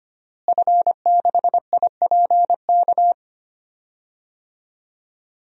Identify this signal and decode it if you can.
Morse code, heard as a keyed tone, about 25 wpm, 700 Hz. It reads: F6IPK
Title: DXCC-100 Call Signs - 5 Characters - Repeated 2x - 25wpm